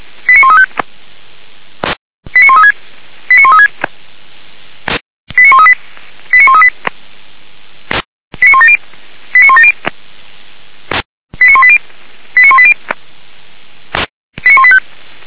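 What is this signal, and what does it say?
Five-tone selective call (ZVEI), heard as quick runs of five tones, tones of about 70 ms. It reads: 89135 89136 89136 89138 89138 89170 89170 89180 89180 89185
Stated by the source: ZVEI five-tone selective call 89134 received by SDR, from feuerwehr-orlamuende.org (Feuerwehr Orlamünde).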